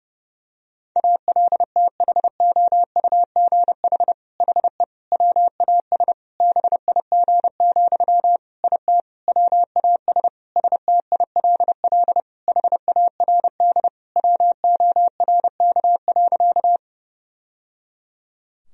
Morse code, heard as a keyed tone, about 31 wpm, 700 Hz. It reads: ALTHOUGH HE WAS BIG, IT WAS STILL HARD WORK.
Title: Sentences from Top 1000 Words - Review Part 3 - 30wpm